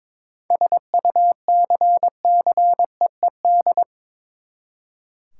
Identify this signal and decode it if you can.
Morse code, heard as a keyed tone, about 22 wpm, 700 Hz. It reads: SUCCEED
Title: Seven Letter Words - 22wpm